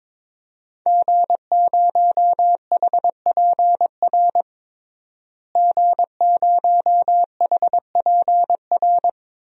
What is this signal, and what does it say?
Morse code, heard as a keyed tone, about 22 wpm, 700 Hz. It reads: G0HPR G0HPR
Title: DXCC-100 Call Signs - 5 Characters - Repeated 3x - 22wpm